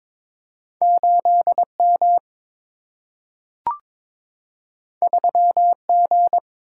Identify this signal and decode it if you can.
Morse code, heard as a keyed tone, about 22 wpm, 700 Hz. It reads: 8M 3G
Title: Single-Letter Single-Number 22wpm